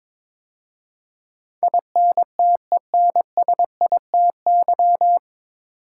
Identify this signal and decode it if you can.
Morse code, heard as a keyed tone, about 22 wpm, 690 Hz. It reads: INTENSITY
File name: 9 Letter Words - First 3 Letters Given - 22wpm